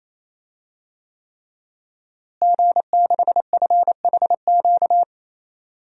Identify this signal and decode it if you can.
Morse code, heard as a keyed tone, about 28 wpm, 700 Hz. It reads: G6FHQ
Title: DXCC-100 Call Signs - 5 Characters - Repeated 3x - 28wpm